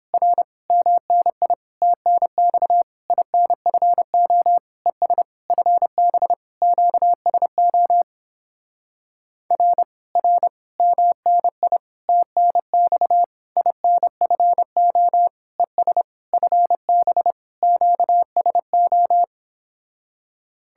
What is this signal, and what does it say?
Morse code, heard as a keyed tone, about 29 wpm, 700 Hz. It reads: R MNI TNX INFO ES FB QSO R R MNI TNX INFO ES FB QSO